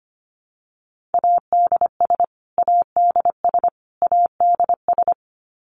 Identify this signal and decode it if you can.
Morse code, heard as a keyed tone, about 25 wpm, 700 Hz. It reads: ADS ADS ADS
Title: Abbreviations Repeated 3x - 25wpm